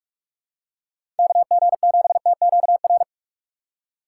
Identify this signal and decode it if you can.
Morse code, heard as a keyed tone, about 46 wpm, 700 Hz. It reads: KG7TQR
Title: USA Call Signs - 6 Characters - 4x Character Spacing - 45wpm